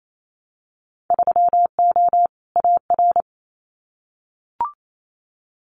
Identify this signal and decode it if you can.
Morse code, heard as a keyed tone, about 28 wpm, 700 Hz. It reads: 3O AR
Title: Winter Field Day Exchanges Encore 28wpm